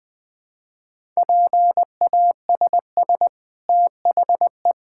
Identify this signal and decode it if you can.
Morse code, heard as a keyed tone, about 20 wpm, 700 Hz. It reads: PASS THE
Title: Sentences from Top 1000 Words - Review Part 5 - 20wpm